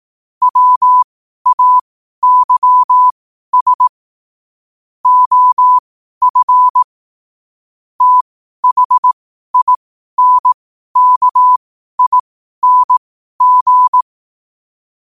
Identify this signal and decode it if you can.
Morse code, heard as a keyed tone, about 18 wpm, spaced as for 12 wpm, 1000 Hz. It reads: WAYS OF THINKING
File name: qotdfiles / 12wpm